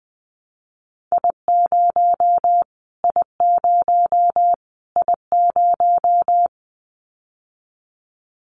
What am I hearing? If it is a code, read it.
Morse code, heard as a keyed tone, about 20 wpm, 700 Hz. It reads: I0 I0 I0
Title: Callsign Prefixes for Top 25 DXCC - Repeated 3x - 20wpm